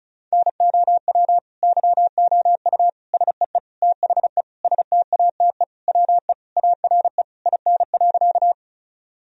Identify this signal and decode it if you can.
Morse code, heard as a keyed tone, about 36 wpm, 700 Hz. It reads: NOW YOU SEE THE STATE WE ARE IN.
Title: Sentences from Top 1000 Words - Review Part 2 - 35wpm